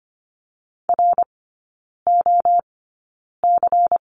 Morse code, heard as a keyed tone, about 25 wpm, spaced as for 9 wpm, 700 Hz. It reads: ROC
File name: course / 25:10wpm / 093 Lesson 16e - Letter c - Three Characters 25wpm